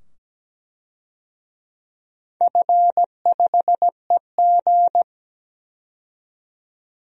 Morse code, heard as a keyed tone, about 17 wpm, 700 Hz. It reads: F5EG
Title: DXCC-100 Call Signs - 4 Characters - 17wpm